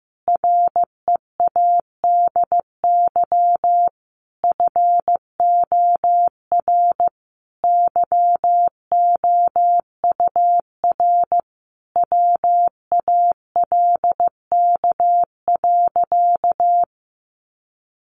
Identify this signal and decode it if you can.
Morse code, heard as a keyed tone, about 15 wpm, 700 Hz. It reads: READY FOR YOUR WALK.
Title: Sentences from Top 800 Words 15wpm